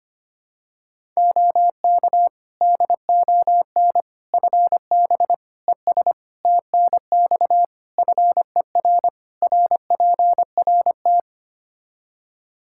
Morse code, heard as a keyed tone, about 25 wpm, 700 Hz. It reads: OK DON FB ES TNX FER RPRT
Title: Practice QSO - Exchange 2 - Info - 25wpm